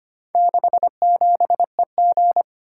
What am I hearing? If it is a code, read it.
Morse code, heard as a keyed tone, about 25 wpm, 700 Hz. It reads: 67EG
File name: practice / 25wpm / Four Letter-Number Combinations - 25wpm